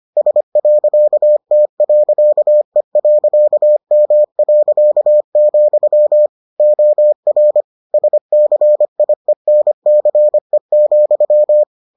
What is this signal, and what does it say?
Morse code, heard as a keyed tone, about 25 wpm, 600 Hz. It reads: S.T.E.M., OR SCIENCE,